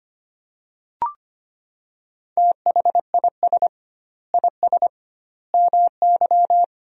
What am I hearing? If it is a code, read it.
Morse code, heard as a keyed tone, about 25 wpm, 700 Hz. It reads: THIS IS MY
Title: 2x Word-Spacing - Sentences from Top 200 Words - 25wpm